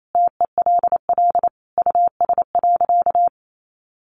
Morse code, heard as a keyed tone, about 28 wpm, 700 Hz. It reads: TELL US.
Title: Sentences from Top 1400 Words 28wpm